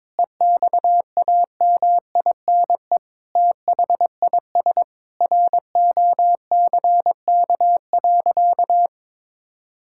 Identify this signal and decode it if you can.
Morse code, heard as a keyed tone, about 21 wpm, 700 Hz. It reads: EXAMINE THIS ROCK.
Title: Sentences from Top 1600 Words 22wpm